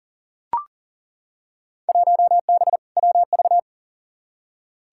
Morse code, heard as a keyed tone, about 37 wpm, 700 Hz. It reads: 1B WV
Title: Field Day Exchanges - 40wpm